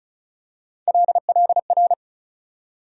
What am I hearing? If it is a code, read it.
Morse code, heard as a keyed tone, about 34 wpm, 700 Hz. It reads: LLR
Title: USA Call Sign Postfixes - 3 Letters 35wpm